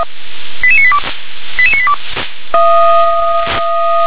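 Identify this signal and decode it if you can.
Five-tone selective call (ZVEI), heard as quick runs of five tones, tones of about 70 ms. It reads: 88982 88982
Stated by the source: ZVEI five-tone selective call received on an SDR.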